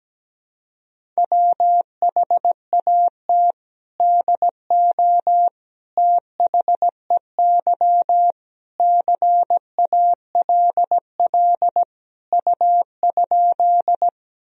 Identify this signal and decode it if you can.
Morse code, heard as a keyed tone, about 17 wpm, 700 Hz. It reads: WHAT DO THEY CALL U?
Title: Sentences from Top 100 Words and Abbreviations 17wpm